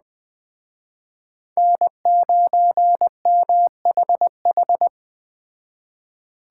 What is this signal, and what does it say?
Morse code, heard as a keyed tone, about 20 wpm, 700 Hz. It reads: N9MHH
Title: Speed Racing - USA 1x3 Call Signs 17wpm